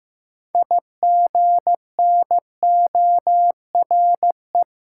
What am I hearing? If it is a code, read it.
Morse code, heard as a keyed tone, about 15 wpm, 700 Hz. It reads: IGNORE
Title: Six Letter Words - 15wpm